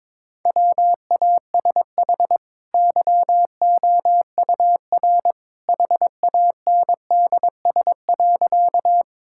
Morse code, heard as a keyed tone, about 23 wpm, 700 Hz. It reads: WASH YOUR HANDS.
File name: practice / 22wpm / Easy Sentences with Hints - 22wpm